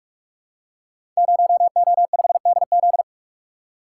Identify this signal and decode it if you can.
Morse code, heard as a keyed tone, about 44 wpm, 700 Hz. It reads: 0O5DZ
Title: Code Groups - 5 Letter:Numbers - 45wpm